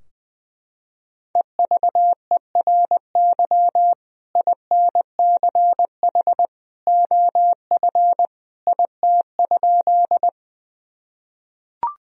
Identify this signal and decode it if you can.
Morse code, heard as a keyed tone, about 20 wpm, 700 Hz. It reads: EVERY INCH OF IT?
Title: Sentences from Top 1000 Words - Review Part 3 - 20wpm